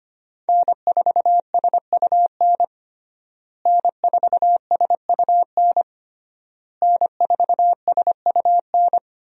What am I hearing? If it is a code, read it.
Morse code, heard as a keyed tone, about 25 wpm, 700 Hz. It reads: N4SUN N4SUN N4SUN
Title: USA Call Signs - 5 Characters - Repeated 3x - 25wpm